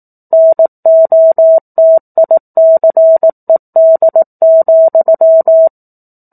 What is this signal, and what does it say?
Morse code, heard as a keyed tone, about 18 wpm, 650 Hz. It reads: NOTICED,
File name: 04-Chapter-04 18 WPM